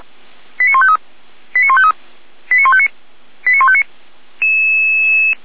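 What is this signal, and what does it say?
Five-tone selective call (ZVEI), heard as quick runs of five tones, tones of about 70 ms. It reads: 89153 89153 89159 89159
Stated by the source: ZVEI five-tone selective call 89150 received by SDR, from the feuerwehr-orlamuende.org fire brigade server.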